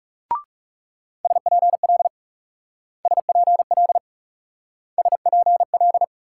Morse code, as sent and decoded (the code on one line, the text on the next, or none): ... .--. .-.. / ... .--. .-.. / ... .--. .-..
SPL SPL SPL